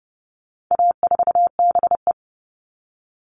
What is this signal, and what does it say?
Morse code, heard as a keyed tone, about 30 wpm, 700 Hz. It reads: A4BE
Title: All Cut Numbers - 4-Digit 30wpm